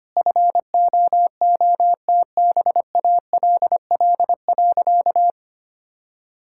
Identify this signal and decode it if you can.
Morse code, heard as a keyed tone, about 25 wpm, 700 Hz. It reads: FOOTBALL.
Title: Sentences from Top 1600 Words 25wpm